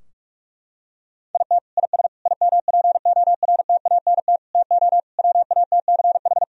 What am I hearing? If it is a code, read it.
Morse code, heard as a keyed tone, about 46 wpm, 700 Hz. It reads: IT IS IMPORTANT TO WATCH
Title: Sentences from Top 1000 Words - Review Part 4 - 45wpm